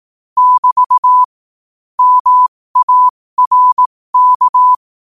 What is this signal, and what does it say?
Morse code, heard as a keyed tone, about 18 wpm, spaced as for 15 wpm, 1000 Hz. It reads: <BT> MARK
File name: qotdfiles / 15wpm